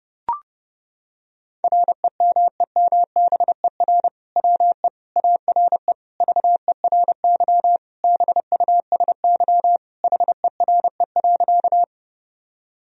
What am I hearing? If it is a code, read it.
Morse code, heard as a keyed tone, about 32 wpm, 700 Hz. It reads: REMEMBER WE ARE VERY BUSY HERE.
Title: Sentences from Top 1000 Words - Review Part 3 - 30wpm